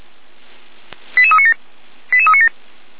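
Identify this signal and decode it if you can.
Five-tone selective call (ZVEI), heard as quick runs of five tones, tones of about 70 ms. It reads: 88397 88397
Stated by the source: ZVEI five-tone selective call received by SDR, from the Feuerwehr Orlamünde (fire brigade) audio server.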